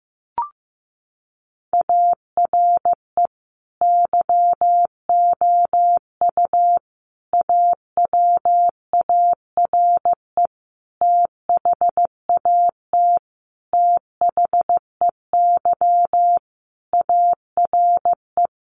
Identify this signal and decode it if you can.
Morse code, heard as a keyed tone, about 15 wpm, 700 Hz. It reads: ARE YOU AWARE THAT THEY ARE
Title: Sentences from Top 1200 Words 15wpm